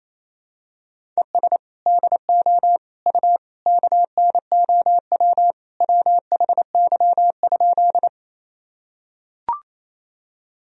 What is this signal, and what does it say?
Morse code, heard as a keyed tone, about 28 wpm, 700 Hz. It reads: ES DO U KNOW WHY?
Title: Sentences from Top 200 Words and Abbreviations 28wpm